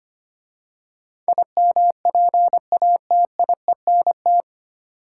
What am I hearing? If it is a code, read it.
Morse code, heard as a keyed tone, about 25 wpm, 700 Hz. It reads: IMPATIENT
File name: practice / 25wpm / 9 Letter Words - First 3 Letters Given - 25wpm